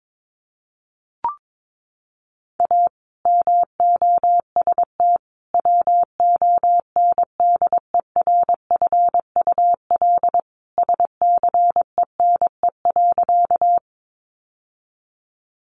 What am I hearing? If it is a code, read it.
Morse code, heard as a keyed tone, about 22 wpm, 700 Hz. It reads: A MOST WONDERFUL SCENE.
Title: Sentences from Top 1100 Words 22wpm